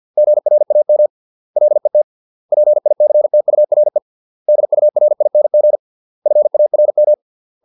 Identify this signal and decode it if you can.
Morse code, heard as a keyed tone, about 50 wpm, 600 Hz. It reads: GRAM LET PICTURE DURING FARM